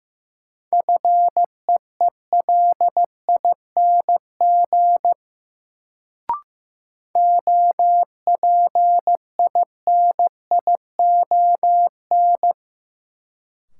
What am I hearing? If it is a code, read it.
Morse code, heard as a keyed tone, about 15 wpm, 700 Hz. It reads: FEELING OPINION
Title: Sentences from Top 1200 Words - New Word Drill Alt - 15wpm